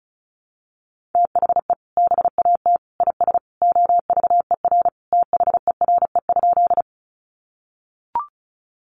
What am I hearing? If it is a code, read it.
Morse code, heard as a keyed tone, about 34 wpm, 700 Hz. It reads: THE BAT IS OVER THERE?
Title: Sentences from Top 1500 Words 35wpm